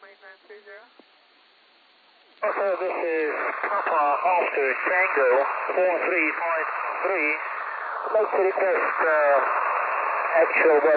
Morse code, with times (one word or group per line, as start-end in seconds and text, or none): none